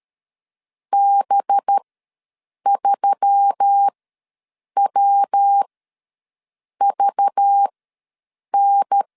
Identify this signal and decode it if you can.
Morse code, heard as a keyed tone, about 13 wpm, spaced as for 7 wpm, 800 Hz. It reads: B3WVN
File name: LEZIONI / 5 LBVC / 5 LBV ws 12wpm 3015